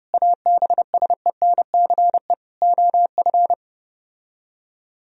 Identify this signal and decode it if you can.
Morse code, heard as a keyed tone, about 32 wpm, 700 Hz. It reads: ABSENCE OF